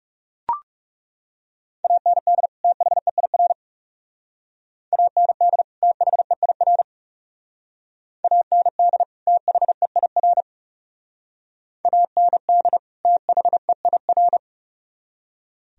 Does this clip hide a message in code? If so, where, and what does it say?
Yes: Morse code from 1.8 to 14.4 s, heard as a keyed tone, about 39 wpm, 700 Hz. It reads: AND THEIR AND THEIR AND THEIR AND THEIR